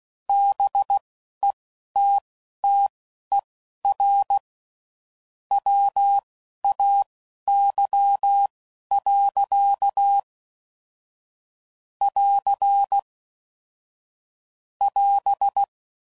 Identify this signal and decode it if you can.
Morse code, heard as a keyed tone, about 16 wpm, spaced as for 11 wpm, 800 Hz. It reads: BETTER WAY. <AR> <AS>
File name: CW Code Groups @10WPM / Technical Subjects / Gates Intro